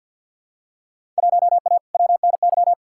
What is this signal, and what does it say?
Morse code, heard as a keyed tone, about 51 wpm, 700 Hz. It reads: 1A WNY
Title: Field Day Exchanges - 50wpm